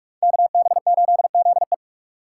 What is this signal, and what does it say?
Morse code, heard as a keyed tone, about 48 wpm, 700 Hz. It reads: KB8ZE